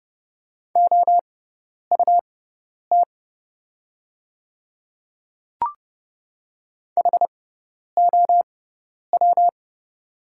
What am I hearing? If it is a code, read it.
Morse code, heard as a keyed tone, about 30 wpm, spaced as for 10 wpm, 700 Hz. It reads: OUT HOW
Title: 6x Character Spacing - Top 100 Words - 30wpm